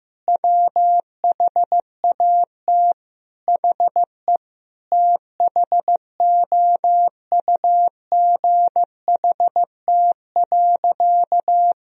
Morse code, heard as a keyed tone, about 15 wpm, 700 Hz. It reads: WHAT HE THOUGHT.
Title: Sentences from Top 1000 Words - Review Part 2 - 15wpm